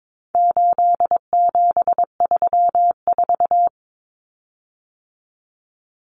Morse code, heard as a keyed tone, about 22 wpm, 700 Hz. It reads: 8734
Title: Common Cut Numbers - 4-Digit 22wpm